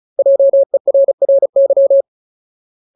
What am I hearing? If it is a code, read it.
Morse code, heard as a keyed tone, about 34 wpm, 550 Hz. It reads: JERRY
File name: Ultimate Word List / 35WPM Ultimate